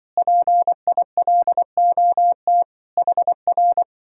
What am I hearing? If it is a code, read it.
Morse code, heard as a keyed tone, about 25 wpm, 700 Hz. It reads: PILOT HR
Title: Speed-Racing Common QSO Phrases - 20wpm